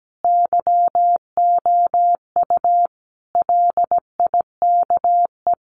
Morse code, heard as a keyed tone, about 17 wpm, 700 Hz. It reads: YOU LIKE
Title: Sentences from Top 200 Words 17wpm